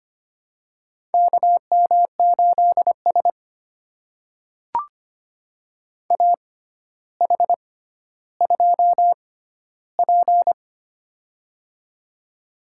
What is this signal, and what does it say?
Morse code, heard as a keyed tone, about 25 wpm, spaced as for 13 wpm, 700 Hz. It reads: KM8S AH2P
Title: USA Call Signs - 4 Characters - 6x Character Spacing - 25wpm